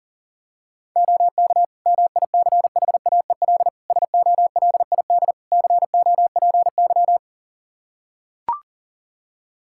Morse code, heard as a keyed tone, about 40 wpm, 700 Hz. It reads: OK MICHAEL SOLID COPY
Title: Practice QSO - Exchange 3 - More Info - 40wpm